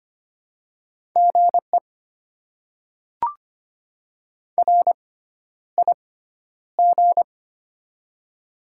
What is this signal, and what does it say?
Morse code, heard as a keyed tone, about 25 wpm, spaced as for 11 wpm, 700 Hz. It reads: GE RIG
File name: Top 20 QSO Elements - 6x Character Spacing - 25wpm